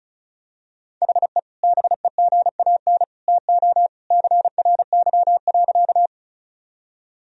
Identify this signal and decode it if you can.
Morse code, heard as a keyed tone, about 34 wpm, 700 Hz. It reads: HE BEGAN TO CRY.
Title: Sentences from Top 1000 Words - Review Part 2 - 35wpm